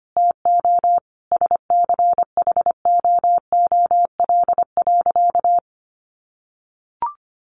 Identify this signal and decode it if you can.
Morse code, heard as a keyed tone, about 25 wpm, 700 Hz. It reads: TO SCHOOL.
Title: Sentences from Top 1400 Words 25wpm